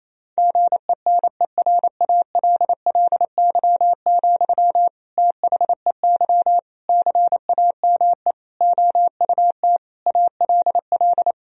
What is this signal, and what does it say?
Morse code, heard as a keyed tone, about 28 wpm, 700 Hz. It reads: GENERALLY, THEY CAME OUT ALL
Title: Sentences from Top 1000 Words - Review Part 4 - 28wpm